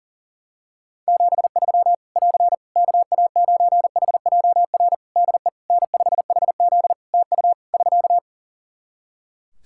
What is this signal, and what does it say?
Morse code, heard as a keyed tone, about 40 wpm, 700 Hz. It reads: 73 <AR> KA9HJR DE N5HZ TU <SK>
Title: Practice QSO - Exchange 4 - Ending - 40wpm